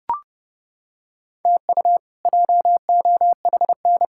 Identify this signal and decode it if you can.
Morse code, heard as a keyed tone, about 30 wpm, 700 Hz. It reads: TU JOHN
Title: K1USN SST QSO Practice - 30wpm